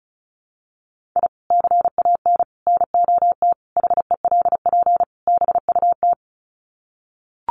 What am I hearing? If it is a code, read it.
Morse code, heard as a keyed tone, about 34 wpm, 700 Hz. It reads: I CAN NOT HELP BUT
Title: Sets of 5 Words Encore2 35wpm